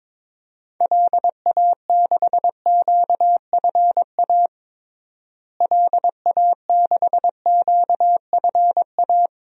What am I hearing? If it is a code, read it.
Morse code, heard as a keyed tone, about 22 wpm, 700 Hz. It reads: LA6QFA LA6QFA
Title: DXCC-100 Call Signs - 6 Characters - Repeated 2x - 22wpm